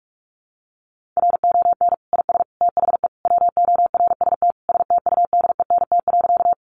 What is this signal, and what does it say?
Morse code, heard as a keyed tone, about 44 wpm, 700 Hz. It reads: RON IS THE WORST STUDENT.
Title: Additional Easy Phrases 45wpm